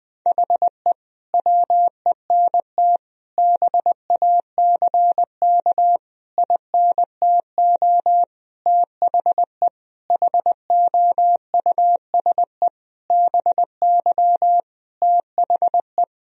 Morse code, heard as a keyed tone, about 20 wpm, 700 Hz. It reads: HE WENT BACK INTO THE HOUSE BY THE
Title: Sentences from Top 1000 Words - Review Part 2 - 20wpm